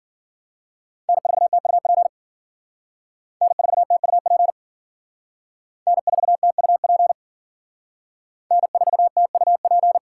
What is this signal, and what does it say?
Morse code, heard as a keyed tone, about 51 wpm, 700 Hz. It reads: N4TUP N4TUP N4TUP N4TUP